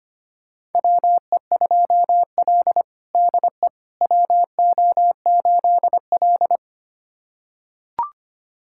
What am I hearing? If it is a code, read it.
Morse code, heard as a keyed tone, about 25 wpm, 700 Hz. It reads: WE2L DE WO8L
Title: USA Callsign DE Callsign - 4 Characters - 25wpm